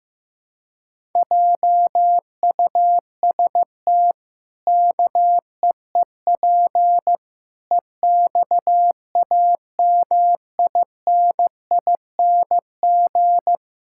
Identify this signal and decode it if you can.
Morse code, heard as a keyed tone, about 15 wpm, 700 Hz. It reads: JUST KEEP EXAMINING
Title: Pangram Challenge v2 - 15wpm